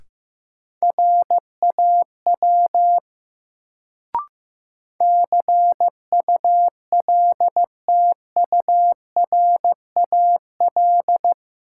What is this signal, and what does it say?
Morse code, heard as a keyed tone, about 15 wpm, 700 Hz. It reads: RAW CULTURAL
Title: Sentences from Top 1500 Words - New Word Drill Alt - 15wpm